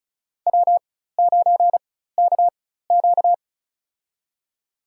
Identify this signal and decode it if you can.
Morse code, heard as a keyed tone, about 36 wpm, spaced as for 16 wpm, 700 Hz. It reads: W9KQ
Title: USA Call Signs - 4 Characters - 4x Character Spacing - 35wpm